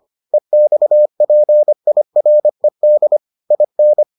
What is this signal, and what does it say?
Morse code, heard as a keyed tone, about 25 wpm, 600 Hz. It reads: EXPIRED IN